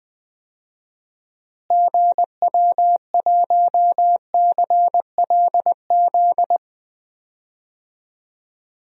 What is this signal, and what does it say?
Morse code, heard as a keyed tone, about 21 wpm, 700 Hz. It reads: GW1CLZ
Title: DXCC-100 Call Signs - 6 Characters - Repeated 3x - 20wpm